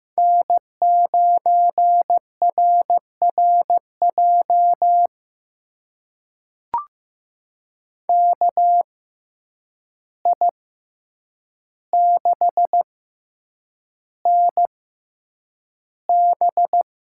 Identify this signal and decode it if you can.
Morse code, heard as a keyed tone, about 15 wpm, spaced as for 8 wpm, 700 Hz. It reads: N9RRJ KI6NB
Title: USA Call Signs - 5 Characters - 6x Character Spacing - 15wpm